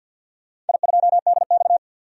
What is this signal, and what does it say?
Morse code, heard as a keyed tone, about 51 wpm, 700 Hz. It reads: I2DX
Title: DXCC-100 Call Signs - 4 Characters - 50wpm